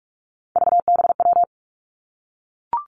Morse code, heard as a keyed tone, about 46 wpm, 700 Hz. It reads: VBW
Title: Three Challenging Letter-Number Combinations 45wpm